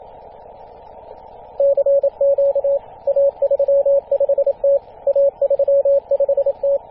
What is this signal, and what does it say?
Morse code, heard as a keyed tone, about 27 wpm, 560 Hz. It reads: CQ A35T A35T
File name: posts / 2016-feb-27-0448z-a35t-24-898mhz-cw